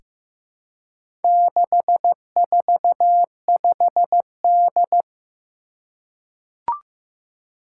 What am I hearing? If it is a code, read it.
Morse code, heard as a keyed tone, about 15 wpm, 700 Hz. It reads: 645D
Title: Four Challenging Letter-Number Combinations 15wpm